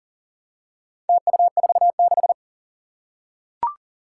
Morse code, heard as a keyed tone, about 38 wpm, 700 Hz. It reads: TU46
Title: Four Challenging Letter-Number Combinations 40wpm